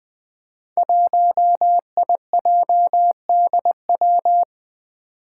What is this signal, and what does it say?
Morse code, heard as a keyed tone, about 20 wpm, 700 Hz. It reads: 1IJDW
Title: Five Challenging Letter-Number Combinations 20wpm